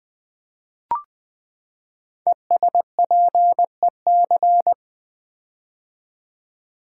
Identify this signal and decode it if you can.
Morse code, heard as a keyed tone, about 20 wpm, 700 Hz. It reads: ESPEC
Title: Build A Word - 5 Letters At A Time - 10-Letter Words - 20wpm